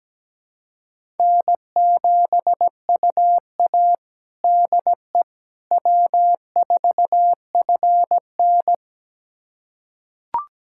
Morse code, heard as a keyed tone, about 17 wpm, 700 Hz. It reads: N7UA DE W4FN
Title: USA Callsign DE Callsign - 4 Characters - 17wpm